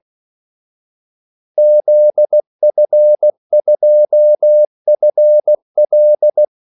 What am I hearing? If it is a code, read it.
Morse code, heard as a keyed tone, about 16 wpm, 600 Hz. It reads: ZF2FL
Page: Kaldesignaler 16 wpm